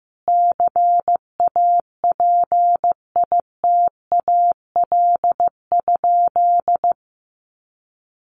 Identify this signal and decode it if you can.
Morse code, heard as a keyed tone, about 15 wpm, 700 Hz. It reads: CAPITAL?
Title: Sentences from Top 1100 Words 15wpm